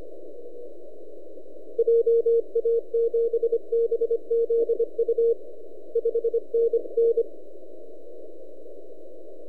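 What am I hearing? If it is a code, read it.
Morse code, heard as a keyed tone, about 24 wpm, 460 Hz. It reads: JA7BZU 5NN
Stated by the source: E6XG heard on 10 m CW, recorded by @n6ml.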